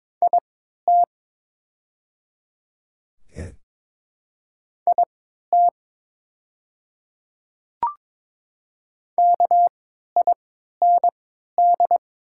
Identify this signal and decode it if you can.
Morse code, heard as a keyed tone, about 22 wpm, spaced as for 12 wpm, 700 Hz. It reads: IT IT KIND
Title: 3x Character Spacing - Top 300 Words - 22wpm